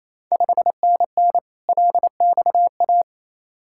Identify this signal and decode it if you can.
Morse code, heard as a keyed tone, about 28 wpm, 700 Hz. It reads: 5NN LXA
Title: Ontario QSO Party Exchange 28wpm